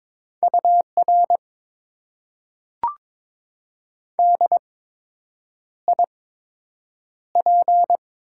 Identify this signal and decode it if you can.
Morse code, heard as a keyed tone, about 22 wpm, 700 Hz. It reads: U R D I P
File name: Top 90 QSO Elements - 8x Character Spacing - 22wpm